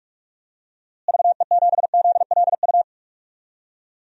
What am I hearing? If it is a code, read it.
Morse code, heard as a keyed tone, about 44 wpm, 700 Hz. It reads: VE7ZLU
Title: DXCC-100 Call Signs - 6 Characters - Repeated 3x - 45wpm